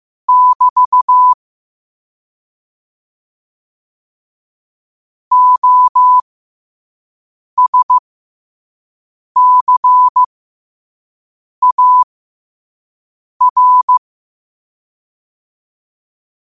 Morse code, heard as a keyed tone, about 15 wpm, spaced as for 5 wpm, 1000 Hz. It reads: <BT> OSCAR
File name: qotdfiles / 5wpm